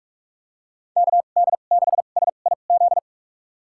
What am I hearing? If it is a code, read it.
Morse code, heard as a keyed tone, about 43 wpm, spaced as for 32 wpm, 700 Hz. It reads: KD6SIZ
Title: USA Call Signs - 6 Characters - 2x Character Spacing - 45wpm